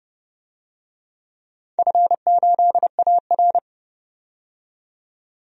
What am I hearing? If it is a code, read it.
Morse code, heard as a keyed tone, about 31 wpm, 700 Hz. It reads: F8AR
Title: DXCC-100 Call Signs - 4 Characters - 30wpm